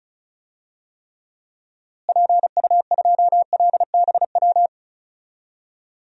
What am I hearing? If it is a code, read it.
Morse code, heard as a keyed tone, about 36 wpm, 700 Hz. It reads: PU2LBW
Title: DXCC-100 Call Signs - 6 Characters - Repeated 3x - 35wpm